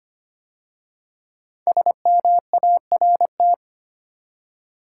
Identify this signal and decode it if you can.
Morse code, heard as a keyed tone, about 25 wpm, 700 Hz. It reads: SMART
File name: 5 Letter Words - First 3 Letters Given - 25wpm